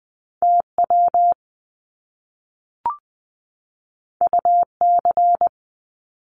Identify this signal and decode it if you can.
Morse code, heard as a keyed tone, about 20 wpm, 700 Hz. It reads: TW UC
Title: Top Two Letter Combinations in Words 20wpm